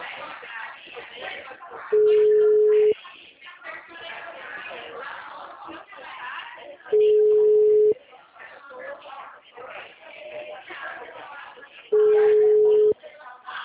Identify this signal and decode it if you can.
Ringback tone (Europe, 425 Hz), with speech.